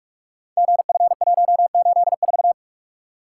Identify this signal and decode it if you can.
Morse code, heard as a keyed tone, about 44 wpm, 690 Hz. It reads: GF184
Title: Code Groups - 5 Letter:Numbers - 45wpm